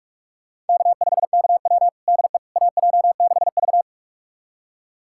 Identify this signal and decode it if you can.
Morse code, heard as a keyed tone, about 46 wpm, 700 Hz. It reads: K5KW DE AJ6V
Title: USA Callsign DE Callsign - 4 Characters - 45wpm